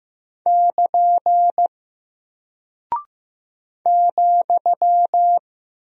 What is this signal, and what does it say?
Morse code, heard as a keyed tone, about 15 wpm, 700 Hz. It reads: <KN> ,
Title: Single Character (Letter:Number:Punctuation:Prosign) Alt 15wpm